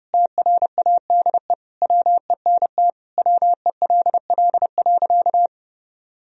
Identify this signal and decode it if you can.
Morse code, heard as a keyed tone, about 30 wpm, 700 Hz. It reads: TRADE WENT WELL.
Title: Sentences from Top 1000 Words - Review Part 4 - 30wpm